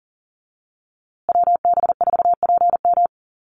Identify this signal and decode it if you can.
Morse code, heard as a keyed tone, about 40 wpm, 700 Hz. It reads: WB4PM
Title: USA Call Signs Repeated 2x 40wpm